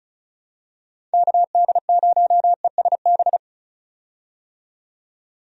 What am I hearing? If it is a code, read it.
Morse code, heard as a keyed tone, about 34 wpm, 700 Hz. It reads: KD0ESB
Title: USA Call Signs - 6 Characters - 35wpm